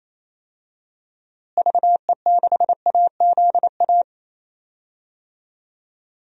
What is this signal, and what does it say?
Morse code, heard as a keyed tone, about 28 wpm, 700 Hz. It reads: VE6AZA
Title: DXCC-100 Call Signs - 6 Characters - Repeated 3x - 28wpm